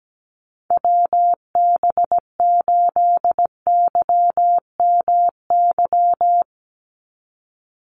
Morse code, heard as a keyed tone, about 17 wpm, 700 Hz. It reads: WB8YMY
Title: USA Call Signs 17wpm